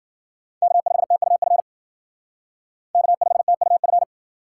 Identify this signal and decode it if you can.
Morse code, heard as a keyed tone, about 52 wpm, 700 Hz. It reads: K5TUF K5TUF